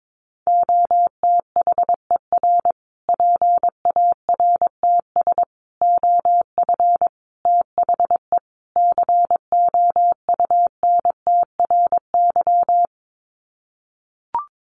Morse code, heard as a keyed tone, about 22 wpm, 700 Hz. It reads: OTHER PARTS OF THE COUNTRY
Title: Sets of 5 Words Encore 22wpm